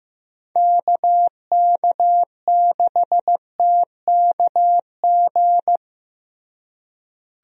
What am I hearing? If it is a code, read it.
Morse code, heard as a keyed tone, about 16 wpm, 700 Hz. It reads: KK6TKG
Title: USA Call Signs - 6 Characters - Repeated 3x - 15wpm